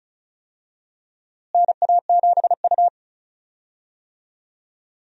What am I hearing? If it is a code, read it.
Morse code, heard as a keyed tone, about 36 wpm, 700 Hz. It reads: NA7U
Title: USA Call Signs - 4 Characters - Repeated 2x - 35wpm